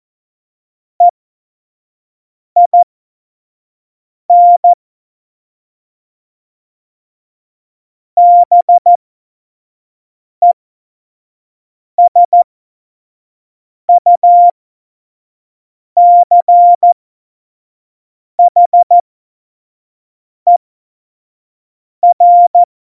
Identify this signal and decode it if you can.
Morse code, heard as a keyed tone, about 14 wpm, spaced as for 5 wpm, 700 Hz. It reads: EIN BESUCHER
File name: Morsekurs / Witze / Witz 29. September 2020, 25 F70